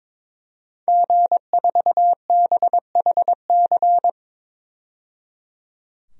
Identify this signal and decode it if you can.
Morse code, heard as a keyed tone, about 22 wpm, 700 Hz. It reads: G4BHC
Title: DXCC-100 Call Signs - 5 Characters - 22wpm